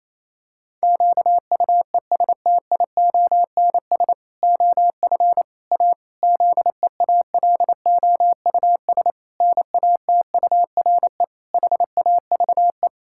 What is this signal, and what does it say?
Morse code, heard as a keyed tone, about 28 wpm, 700 Hz. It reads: QUESTIONS OF A ZEALOUS NATURE HAVE
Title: Pangram Challenge v2 - 28wpm